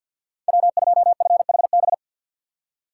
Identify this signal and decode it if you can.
Morse code, heard as a keyed tone, about 52 wpm, 690 Hz. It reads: W2FHB
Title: USA Call Signs - 3x Character Spacing - 50wpm